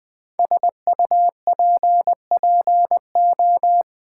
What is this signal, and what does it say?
Morse code, heard as a keyed tone, about 20 wpm, 700 Hz. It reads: SUPPO
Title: Build A Word - 5 Letters At A Time - 10-Letter Words - 20wpm